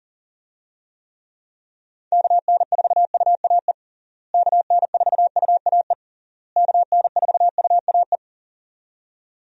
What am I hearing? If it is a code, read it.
Morse code, heard as a keyed tone, about 40 wpm, 700 Hz. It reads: KN4UAE KN4UAE KN4UAE
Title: USA Call Signs - 6 Characters - Repeated 3x - 40wpm